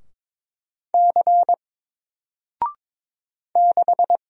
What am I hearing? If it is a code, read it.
Morse code, heard as a keyed tone, about 22 wpm, 700 Hz. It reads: C 6
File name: Single-Letter-Number-22wpm